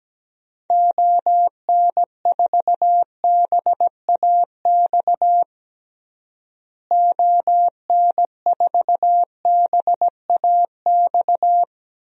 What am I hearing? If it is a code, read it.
Morse code, heard as a keyed tone, about 17 wpm, 700 Hz. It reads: ON4BAX ON4BAX
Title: DXCC-100 Call Signs - 6 Characters - Repeated 2x - 17wpm